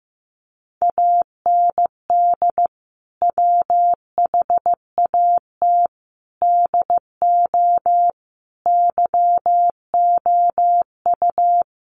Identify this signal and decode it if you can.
Morse code, heard as a keyed tone, about 14 wpm, 700 Hz. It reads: AND WHAT DO YOU
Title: Sentences from Top 100 Words 15wpm